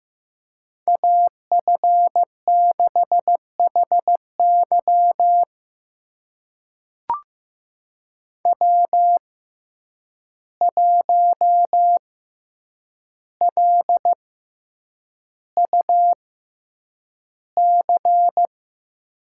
Morse code, heard as a keyed tone, about 15 wpm, spaced as for 8 wpm, 700 Hz. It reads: AF6HY W1LUC
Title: USA Call Signs - 5 Characters - 6x Character Spacing - 15wpm